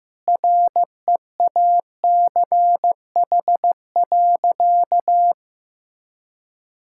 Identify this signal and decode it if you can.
Morse code, heard as a keyed tone, about 15 wpm, 700 Hz. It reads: REACH.
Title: Sentences from Top 1000 Words - Review Part 6 - 15wpm